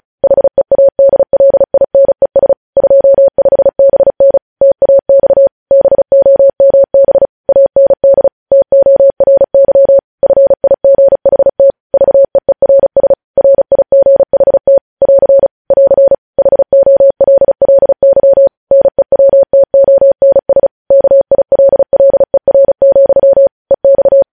Morse code, heard as a keyed tone, about 36 wpm, 550 Hz. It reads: HEADLINES 25BN TAX BOMB AND TORY FIGHT VEERS RIGHT <AR> <AR> HOLLY NEWTONS KILLER, EX